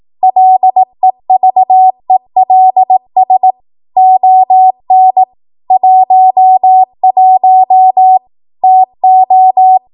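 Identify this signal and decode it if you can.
Morse code, heard as a keyed tone, about 18 wpm, 750 Hz. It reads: LEVELS ON 11 TO
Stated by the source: ARRL 18 wpm Morse code practice archive transmission.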